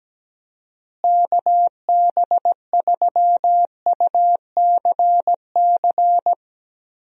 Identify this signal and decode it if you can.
Morse code, heard as a keyed tone, about 17 wpm, 700 Hz. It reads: KB3UCC
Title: USA Call Signs - 6 Characters - Repeated 3x - 17wpm